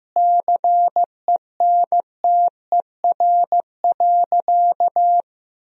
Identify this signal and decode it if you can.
Morse code, heard as a keyed tone, about 15 wpm, 700 Hz. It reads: CENTER.
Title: Sentences from Top 1000 Words - Review Part 2 - 15wpm